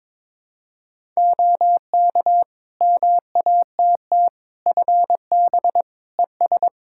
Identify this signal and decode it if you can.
Morse code, heard as a keyed tone, about 21 wpm, 700 Hz. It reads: OK MATT FB ES
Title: Practice QSO - Exchange 2 - Info - 22wpm